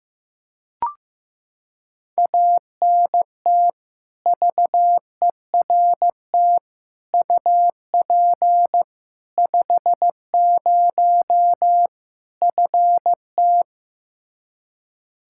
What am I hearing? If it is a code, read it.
Morse code, heard as a keyed tone, about 15 wpm, 700 Hz. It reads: ANT VERT UP 50 FT
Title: Practice QSO - Exchange 2 - Info - 15wpm